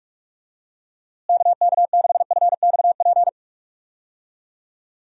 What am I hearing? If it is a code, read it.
Morse code, heard as a keyed tone, about 46 wpm, 700 Hz. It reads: KK6FXP
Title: USA Call Signs - 6 Characters - Repeated 2x - 45wpm